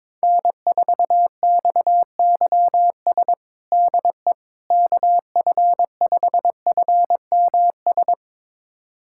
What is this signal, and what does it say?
Morse code, heard as a keyed tone, about 22 wpm, 700 Hz. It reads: N4XYS DE KF5FMS
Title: USA Callsign DE Callsign 22wpm